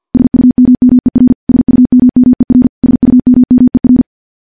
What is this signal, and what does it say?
Morse code, heard as a keyed tone, about 50 wpm, 250 Hz. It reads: SUMMER SUMMER SUMMER
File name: morse / morse-training-50WPM